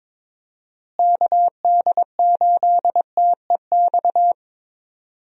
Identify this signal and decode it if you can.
Morse code, heard as a keyed tone, about 22 wpm, 700 Hz. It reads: KD8TEX
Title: USA Call Signs - 6 Characters - Repeated 3x - 22wpm